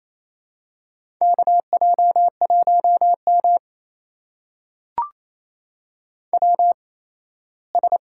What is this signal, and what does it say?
Morse code, heard as a keyed tone, about 28 wpm, spaced as for 17 wpm, 700 Hz. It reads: KJ1M WS
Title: USA Call Signs - 4 Characters - 8x Character Spacing - 28wpm